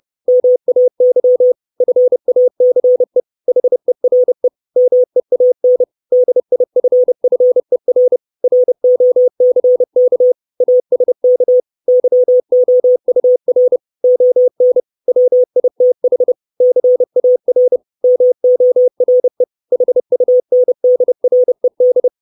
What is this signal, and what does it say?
Morse code, heard as a keyed tone, about 30 wpm, 500 Hz. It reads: MAY FACE HERE MEAN DIFFER ROCK ASK YOUR ON WITH CAR MORE HUNDRED
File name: 632 Words 30WPM